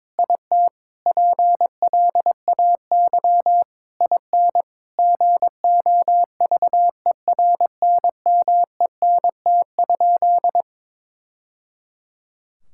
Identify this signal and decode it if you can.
Morse code, heard as a keyed tone, about 23 wpm, 700 Hz. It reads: IT PLAY IN GOVERNMENT?
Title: Sentences from Top 1000 Words - Review Part 6 - 22wpm